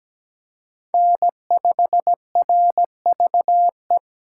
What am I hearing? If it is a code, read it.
Morse code, heard as a keyed tone, about 20 wpm, 700 Hz. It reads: N5RVE
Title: Speed Racing - USA 1x3 Call Signs 17wpm